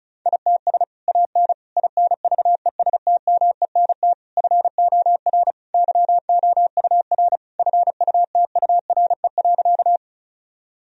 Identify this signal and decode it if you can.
Morse code, heard as a keyed tone, about 34 wpm, 700 Hz. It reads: ITS AN INVESTMENT FOR YOUR FUTURE.
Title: Sentences from Top 1000 Words - Review Part 4 - 35wpm